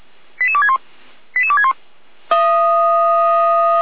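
Five-tone selective call (ZVEI), heard as quick runs of five tones, tones of about 70 ms. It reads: 88261 88261
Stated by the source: Five-tone ZVEI selective call received by SDR, from the Feuerwehr Orlamünde (fire brigade) audio server.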